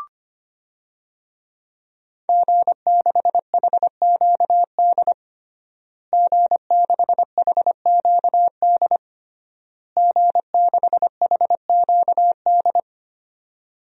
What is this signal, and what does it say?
Morse code, heard as a keyed tone, about 25 wpm, 700 Hz. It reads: G6HQD G6HQD G6HQD
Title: DXCC-100 Call Signs - 5 Characters - Repeated 3x - 25wpm